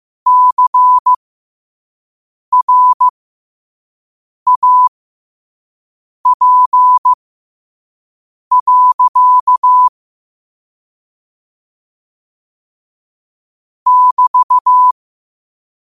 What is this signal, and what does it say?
Morse code, heard as a keyed tone, about 15 wpm, spaced as for 5 wpm, 1000 Hz. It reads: CRAP. <BT>